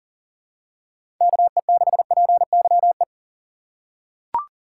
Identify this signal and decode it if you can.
Morse code, heard as a keyed tone, about 37 wpm, 690 Hz. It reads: KE6PYE
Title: USA Call Signs - 6 Characters - 2x Character Spacing - 40wpm